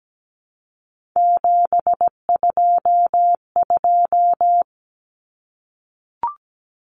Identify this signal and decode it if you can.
Morse code, heard as a keyed tone, about 17 wpm, 700 Hz. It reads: 722
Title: Common Cut Numbers - 3-Digit 17wpm